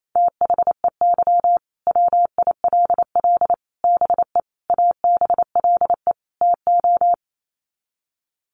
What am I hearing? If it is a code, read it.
Morse code, heard as a keyed tone, about 28 wpm, 700 Hz. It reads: THEY WILL BE ABLE TO
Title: Sets of 5 Words Encore 28wpm